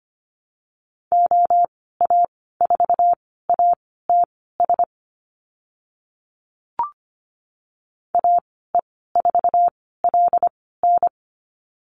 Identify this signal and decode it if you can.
Morse code, heard as a keyed tone, about 25 wpm, spaced as for 16 wpm, 700 Hz. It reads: OA4ATS AE4LN
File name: Lesson 15g - Number 5 - Call Signs 25wpm